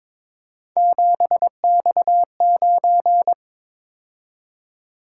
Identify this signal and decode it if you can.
Morse code, heard as a keyed tone, about 22 wpm, 700 Hz. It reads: 7X9
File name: practice / 22wpm / Three Letter-Number Combinations - 22wpm